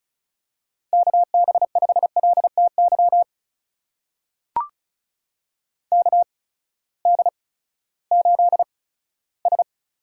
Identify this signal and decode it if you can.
Morse code, heard as a keyed tone, about 36 wpm, spaced as for 18 wpm, 700 Hz. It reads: KB5LTY KD8S
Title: USA Call Signs - 6 Characters - 8x Character Spacing - 35wpm